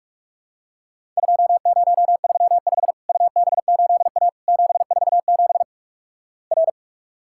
Morse code, heard as a keyed tone, about 47 wpm, 700 Hz. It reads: 2035 UB8A 747 R
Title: Worked All Europe DX Contest - QTC Exchange - 45wpm